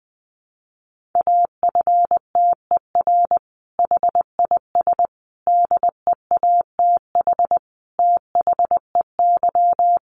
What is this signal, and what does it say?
Morse code, heard as a keyed tone, about 20 wpm, 700 Hz. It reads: AFTER HIS DEATH THEY
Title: Sentences from Top 1100 Words 20wpm